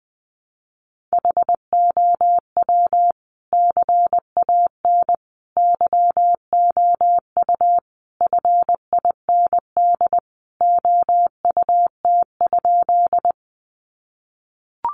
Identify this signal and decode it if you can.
Morse code, heard as a keyed tone, about 20 wpm, 700 Hz. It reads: HOW CAN YOU FIND OUT?
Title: Sentences from Top 200 Words 20wpm